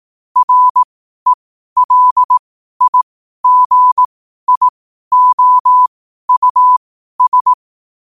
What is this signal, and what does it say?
Morse code, heard as a keyed tone, about 18 wpm, spaced as for 12 wpm, 1000 Hz. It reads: RELIGIOUS
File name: qotdfiles / 12wpm